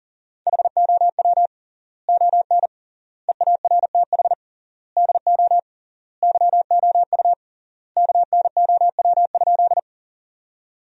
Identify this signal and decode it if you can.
Morse code, heard as a keyed tone, about 37 wpm, 700 Hz. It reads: HOW ON EARTH DO YOU KNOW?
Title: 3x Word-Spacing - Sentences from Top 200 Words - 40wpm